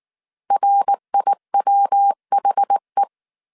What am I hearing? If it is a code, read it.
Morse code, heard as a keyed tone, about 19 wpm, 800 Hz. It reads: RIWHE
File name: CW / DET / 1 3 1500C S B 18wpm 1916